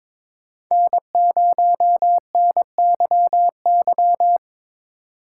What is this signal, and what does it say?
Morse code, heard as a keyed tone, about 22 wpm, 700 Hz. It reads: N0NYY
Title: USA Call Signs - 5 Characters - 2x Character Spacing - 22wpm